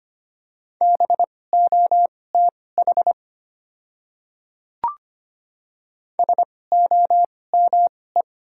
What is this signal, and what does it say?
Morse code, heard as a keyed tone, about 25 wpm, spaced as for 18 wpm, 690 Hz. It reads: BOTH SOME
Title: 2x Character Spacing - Top 300 Words - 25wpm